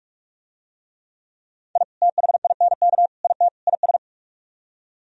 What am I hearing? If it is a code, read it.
Morse code, heard as a keyed tone, about 44 wpm, 700 Hz. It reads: I THINK IT IS